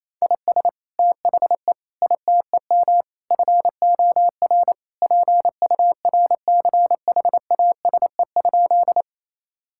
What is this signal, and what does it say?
Morse code, heard as a keyed tone, about 28 wpm, 700 Hz. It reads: IS THE ITEM FOR PURCHASE?
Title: Sentences from Top 1000 Words - Review Part 6 - 28wpm